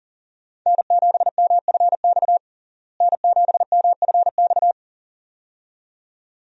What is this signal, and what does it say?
Morse code, heard as a keyed tone, about 40 wpm, 700 Hz. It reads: N7MFX N7MFX
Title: USA Call Signs - 5 Characters - Repeated 2x - 40wpm